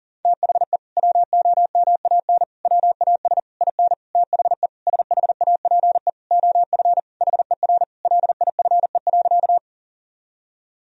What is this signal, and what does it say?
Morse code, heard as a keyed tone, about 40 wpm, 700 Hz. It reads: THE WOMAN WAS IN THE SHAPE OF HER LIFE.
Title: Sentences from Top 1000 Words - Review Part 3 - 40wpm